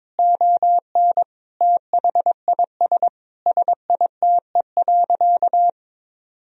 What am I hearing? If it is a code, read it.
Morse code, heard as a keyed tone, about 22 wpm, 700 Hz. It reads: ON THIS SITE.
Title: Sentences from Top 1000 Words - Review Part 4 - 22wpm